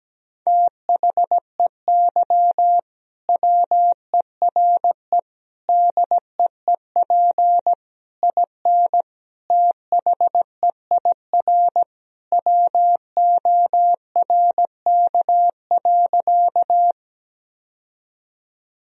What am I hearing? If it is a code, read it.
Morse code, heard as a keyed tone, about 17 wpm, 700 Hz. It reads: THEY WERE DEEP IN THEIR WORK.
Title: Sentences from Top 1000 Words - Review Part 3 - 17wpm